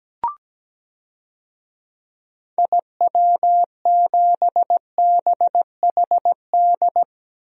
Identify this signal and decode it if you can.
Morse code, heard as a keyed tone, about 17 wpm, 700 Hz. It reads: IW7BHD
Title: DXCC-100 Call Signs - 6 Characters - Repeated 3x - 17wpm